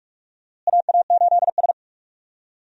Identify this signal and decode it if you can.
Morse code, heard as a keyed tone, about 44 wpm, 700 Hz. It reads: AA8S
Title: USA Call Signs - 4 Characters - Repeated 3x - 45wpm